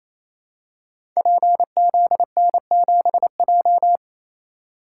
Morse code, heard as a keyed tone, about 28 wpm, 690 Hz. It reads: PZN7J
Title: Code Groups - 5 Letter:Numbers - 28wpm